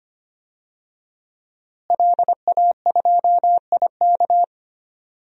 Morse code, heard as a keyed tone, about 25 wpm, 700 Hz. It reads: LA2IK
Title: DXCC-100 Call Signs - 5 Characters - Repeated 2x - 25wpm